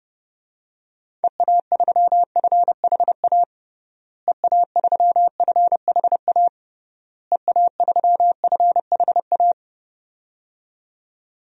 Morse code, heard as a keyed tone, about 30 wpm, 700 Hz. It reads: EA3FHA EA3FHA EA3FHA
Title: DXCC-100 Call Signs - 6 Characters - Repeated 3x - 30wpm